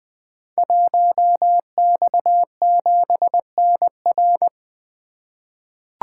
Morse code, heard as a keyed tone, about 20 wpm, 700 Hz. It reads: 1X7NR
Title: Code Groups - 5 Letter:Numbers - 20wpm